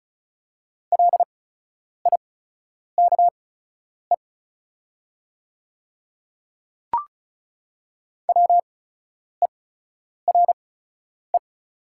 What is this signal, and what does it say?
Morse code, heard as a keyed tone, about 36 wpm, spaced as for 10 wpm, 690 Hz. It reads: LIKE WERE
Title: 8x Character Spacing - Top 200 Words - 35wpm